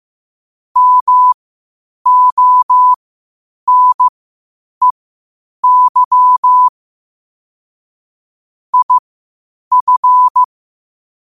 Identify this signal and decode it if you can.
Morse code, heard as a keyed tone, about 15 wpm, spaced as for 8 wpm, 1000 Hz. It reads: MONEY IF
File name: qotdfiles / 8wpm